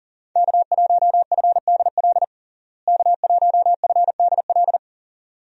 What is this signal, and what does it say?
Morse code, heard as a keyed tone, about 40 wpm, 700 Hz. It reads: K1FDL K1FDL
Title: USA Call Signs - 5 Characters - Repeated 2x - 40wpm